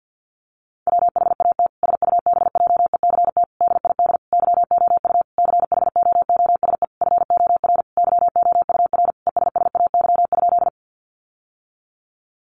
Morse code, heard as a keyed tone, about 50 wpm, 700 Hz. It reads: WHAT SUBJECT DID YOU CHOOSE FOR YOUR ESSAY?
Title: Sentences from Top 1500 Words 50wpm